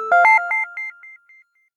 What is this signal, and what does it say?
There is no code.